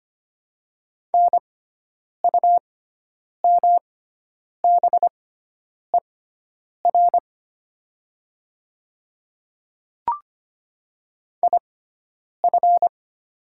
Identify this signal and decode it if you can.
Morse code, heard as a keyed tone, about 25 wpm, spaced as for 9 wpm, 700 Hz. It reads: NUMBER IF
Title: 6x Character Spacing - Top 100 Words - 25wpm